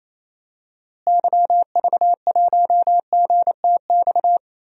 Morse code, heard as a keyed tone, about 28 wpm, 700 Hz. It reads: YV1GTX
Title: DXCC-100 Call Signs - 6 Characters - Repeated 2x - 28wpm